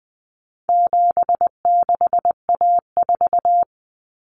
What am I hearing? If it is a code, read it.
Morse code, heard as a keyed tone, about 20 wpm, 700 Hz. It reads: 76A4
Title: Common Cut Numbers - 4-Digit 20wpm